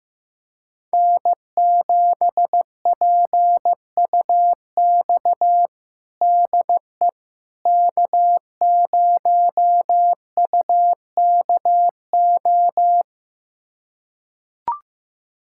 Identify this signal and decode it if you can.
Morse code, heard as a keyed tone, about 15 wpm, 700 Hz. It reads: N7PUX DE K0UKO
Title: USA Callsign DE Callsign - 5 Characters - 15wpm